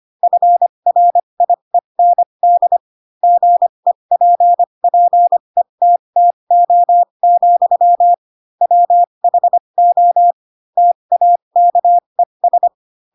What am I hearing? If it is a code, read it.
Morse code, heard as a keyed tone, about 25 wpm, spaced as for 22 wpm, 700 Hz. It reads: FRIEND GEPPETTO, WHO TAKES
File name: Pinocchio / Pinocchio ch2 18 25 ik1hge